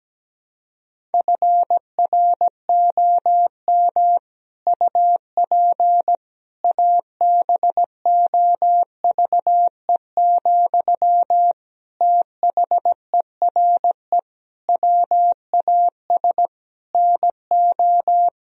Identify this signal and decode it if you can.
Morse code, heard as a keyed tone, about 17 wpm, 700 Hz. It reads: FROM UP ABOVE, THERE WAS NO
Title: Sentences from Top 1000 Words - Review Part 2 - 17wpm